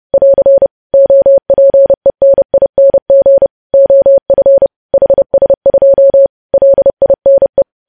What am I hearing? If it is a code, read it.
Morse code, heard as a keyed tone, about 30 wpm, 550 Hz. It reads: <AR> OPENING OF HS2 LINE
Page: news headlines in morse code at 30 wpm